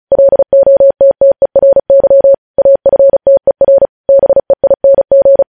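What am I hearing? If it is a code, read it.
Morse code, heard as a keyed tone, about 36 wpm, 550 Hz. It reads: LOTTERY AFTER BEING
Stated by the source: News headlines sent in Morse at 35 wpm.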